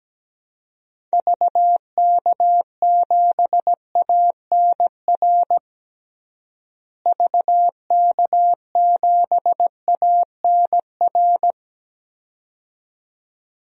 Morse code, heard as a keyed tone, about 17 wpm, 700 Hz. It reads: VK7ANR VK7ANR
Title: DXCC-100 Call Signs - 6 Characters - Repeated 2x - 17wpm